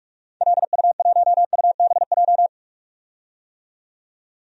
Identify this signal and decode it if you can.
Morse code, heard as a keyed tone, about 43 wpm, 700 Hz. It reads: LU1UBJ
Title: DXCC-100 Call Signs - 6 Characters - 45wpm